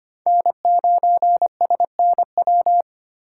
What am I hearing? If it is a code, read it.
Morse code, heard as a keyed tone, about 25 wpm, 700 Hz. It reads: N9SNW
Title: USA Call Signs - 5 Characters - 3x Character Spacing - 25wpm